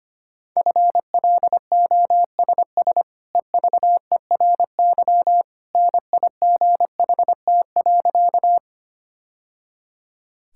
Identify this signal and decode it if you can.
Morse code, heard as a keyed tone, about 25 wpm, 690 Hz. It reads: FLOSS EVERY NIGHT.
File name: Easy Sentences with Hints - 25wpm